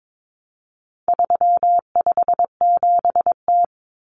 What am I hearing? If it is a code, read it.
Morse code, heard as a keyed tone, about 22 wpm, 700 Hz. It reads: 357T
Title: Common Cut Numbers - 4-Digit 22wpm